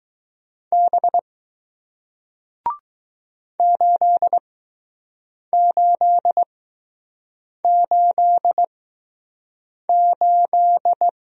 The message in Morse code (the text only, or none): B 8888